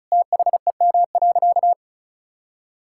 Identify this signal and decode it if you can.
Morse code, heard as a keyed tone, about 36 wpm, 700 Hz. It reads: THEM.
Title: Sentences from Top 1000 Words - Review Part 5 - 35wpm